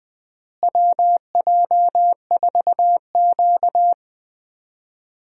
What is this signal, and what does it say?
Morse code, heard as a keyed tone, about 20 wpm, 700 Hz. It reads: WJ4Q
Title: USA Call Signs - 4 Characters - Repeated 3x - 20wpm